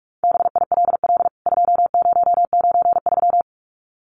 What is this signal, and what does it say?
Morse code, heard as a keyed tone, about 44 wpm, 700 Hz. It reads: BILL 2093
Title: CWT Exchange 45wpm